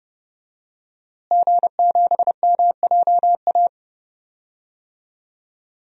Morse code, heard as a keyed tone, about 30 wpm, 700 Hz. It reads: G7MJA
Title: DXCC-100 Call Signs - 5 Characters - Repeated 2x - 30wpm